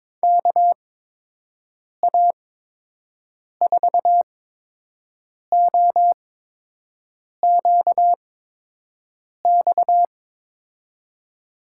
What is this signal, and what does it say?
Morse code, heard as a keyed tone, about 22 wpm, spaced as for 6 wpm, 700 Hz. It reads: KA4OQX
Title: USA Call Signs - 8x Character Spacing - 22wpm